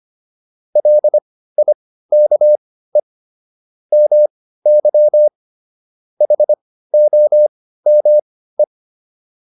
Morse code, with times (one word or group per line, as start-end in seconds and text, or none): 0.8-3.0 LIKE
3.9-5.3 MY
6.2-8.6 HOME